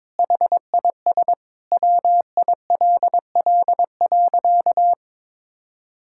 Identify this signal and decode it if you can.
Morse code, heard as a keyed tone, about 21 wpm, 700 Hz. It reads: HIS WILL.
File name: Sentences with 9-Word Phrases - 22wpm